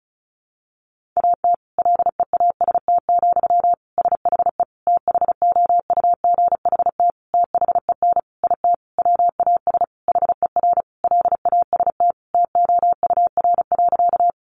Morse code, heard as a keyed tone, about 34 wpm, 700 Hz. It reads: AT LEAST, SHE THOUGHT THEN IT WAS HER LAST TOUR.
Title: Sentences from Top 1400 Words 35wpm